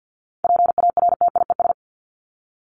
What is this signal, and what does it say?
Morse code, heard as a keyed tone, about 50 wpm, 700 Hz. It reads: PARTIES
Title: ES:S Suffix - Top 100 Words with Base Words 50wpm